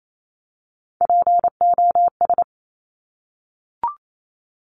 Morse code, heard as a keyed tone, about 28 wpm, 700 Hz. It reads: POS
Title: Top Three Letter Combinations in Words 28wpm